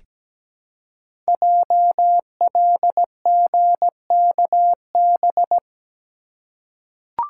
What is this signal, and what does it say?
Morse code, heard as a keyed tone, about 17 wpm, 700 Hz. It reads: JLGKB
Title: Code Groups - 5 Letters - 17wpm